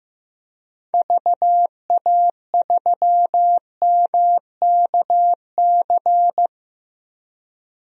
Morse code, heard as a keyed tone, about 15 wpm, 700 Hz. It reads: VA3MKC
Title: DXCC-100 Call Signs - 6 Characters - Repeated 3x - 15wpm